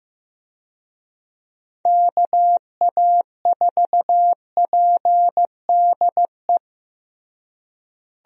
Morse code, heard as a keyed tone, about 16 wpm, 700 Hz. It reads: KA4PDE
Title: USA Call Signs - 6 Characters - Repeated 2x - 15wpm